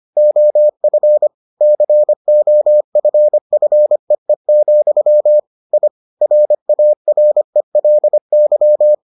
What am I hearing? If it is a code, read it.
Morse code, heard as a keyed tone, about 25 wpm, 600 Hz. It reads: OF COFFEE, I RARELY